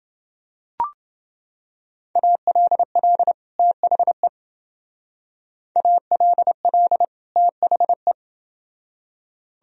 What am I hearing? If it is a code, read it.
Morse code, heard as a keyed tone, about 28 wpm, 700 Hz. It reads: ALL THE ALL THE